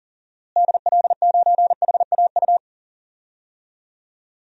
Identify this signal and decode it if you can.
Morse code, heard as a keyed tone, about 40 wpm, 700 Hz. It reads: DL9HAU
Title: DXCC-100 Call Signs - 6 Characters - Repeated 3x - 40wpm